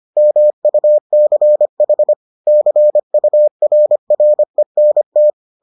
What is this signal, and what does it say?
Morse code, heard as a keyed tone, about 25 wpm, 600 Hz. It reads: MUCH CURRENT